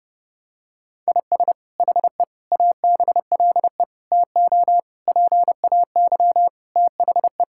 Morse code, heard as a keyed tone, about 30 wpm, 700 Hz. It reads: IS HE ABLE TO PAY THE
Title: Sentences from Top 1000 Words - Review Part 5 - 30wpm